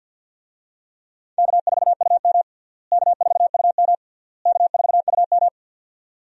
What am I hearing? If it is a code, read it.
Morse code, heard as a keyed tone, about 51 wpm, 700 Hz. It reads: K4UM K4UM K4UM
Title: USA Call Signs - 4 Characters - Repeated 3x - 50wpm